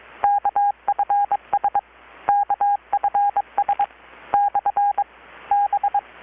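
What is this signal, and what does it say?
Morse code, heard as a keyed tone, about 21 wpm, spaced as for 17 wpm, 810 Hz. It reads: KFS KFS/B
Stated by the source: Morse from KPH's Night of Nights XX maritime radiotelegraphy event, received in Chiapas, Mexico about 3,800 km away on a Kenwood R-600 with a grounded 10 m long-wire antenna.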